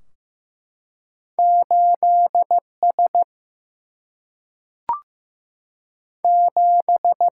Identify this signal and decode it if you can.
Morse code, heard as a keyed tone, about 15 wpm, 700 Hz. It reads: 8S 7
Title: Single-Letter Single-Number 15wpm